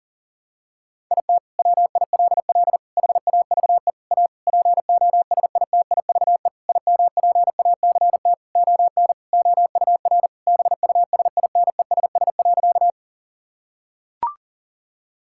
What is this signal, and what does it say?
Morse code, heard as a keyed tone, about 40 wpm, 700 Hz. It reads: IT WILL HAVE A POSITIVE IMPACT ON OUR BUSINESS.
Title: Sentences from Top 1000 Words - Review Part 6 - 40wpm